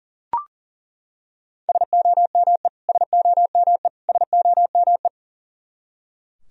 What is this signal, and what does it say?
Morse code, heard as a keyed tone, about 38 wpm, 700 Hz. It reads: SOME SOME SOME
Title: Top 100 Words Repeated 40wpm